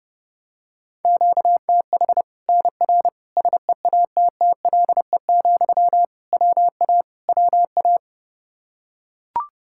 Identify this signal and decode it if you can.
Morse code, heard as a keyed tone, about 32 wpm, 700 Hz. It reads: QTH NR SEATTLE, WA WA
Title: Practice QSO - Exchange 1 - Beginning - 30wpm